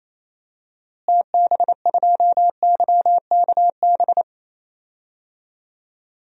Morse code, heard as a keyed tone, about 28 wpm, 700 Hz. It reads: TB2YKB
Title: DXCC-100 Call Signs - 6 Characters - Repeated 2x - 28wpm